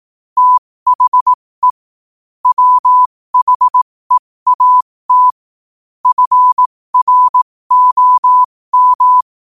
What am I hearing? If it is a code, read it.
Morse code, heard as a keyed tone, about 18 wpm, spaced as for 15 wpm, 1000 Hz. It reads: THE WHEAT FROM
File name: qotdfiles / 15wpm